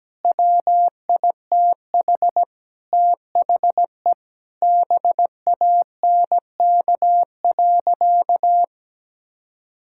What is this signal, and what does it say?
Morse code, heard as a keyed tone, about 17 wpm, 700 Hz. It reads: WITH THE BANK.
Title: Sentences from Top 1000 Words - Review Part 6 - 17wpm